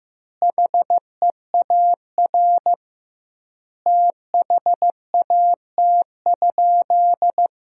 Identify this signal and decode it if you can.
Morse code, heard as a keyed tone, about 15 wpm, 700 Hz. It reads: HEAR THAT?
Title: 2x Word-Spacing - Sentences from Top 300 Words - 15wpm